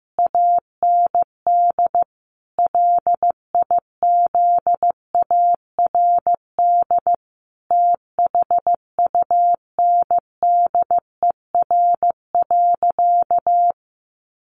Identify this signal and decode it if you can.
Morse code, heard as a keyed tone, about 14 wpm, 700 Hz. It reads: AND LIZARD THUNDER.
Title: A Sound Of Thunder 15wpm